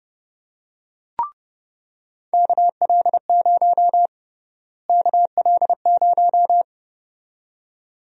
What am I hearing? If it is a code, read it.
Morse code, heard as a keyed tone, about 30 wpm, 700 Hz. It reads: KL0 KL0
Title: USA Call Sign Prefixes - Repeated 2x - 30wpm